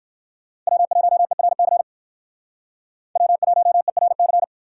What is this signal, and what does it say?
Morse code, heard as a keyed tone, about 59 wpm, 690 Hz. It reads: W1ERC W1ERC